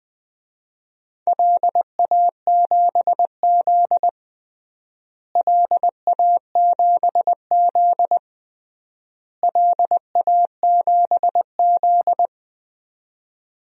Morse code, heard as a keyed tone, about 20 wpm, 700 Hz. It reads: LA7Z LA7Z LA7Z
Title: DXCC-100 Call Signs - 4 Characters - Repeated 3x - 20wpm